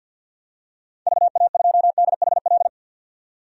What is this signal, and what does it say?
Morse code, heard as a keyed tone, about 52 wpm, 700 Hz. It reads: VA2DHL
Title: DXCC-100 Call Signs - 6 Characters - 50wpm